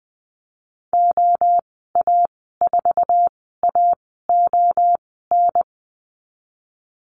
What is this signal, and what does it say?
Morse code, heard as a keyed tone, about 20 wpm, spaced as for 14 wpm, 700 Hz. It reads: OA4AON
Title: 048 Lesson 9g - Number 4 - Call Signs 20wpm